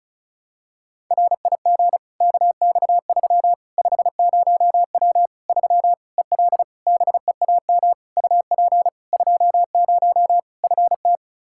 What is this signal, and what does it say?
Morse code, heard as a keyed tone, about 36 wpm, 700 Hz. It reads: RIG KX3 50W 3 EL BEAM UP 20 FT
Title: Practice QSO - Exchange 2 - Info - 35wpm